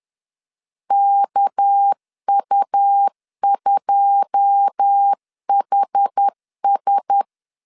Morse code, heard as a keyed tone, about 10 wpm, 800 Hz. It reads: KU2HS
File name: CW / DET / 1 9 1500C S A 10wpm 4511